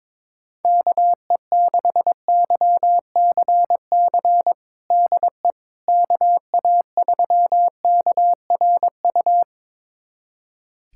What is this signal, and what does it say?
Morse code, heard as a keyed tone, about 22 wpm, 700 Hz. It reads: KE6YCC DE KA3KRU
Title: USA Callsign DE Callsign 22wpm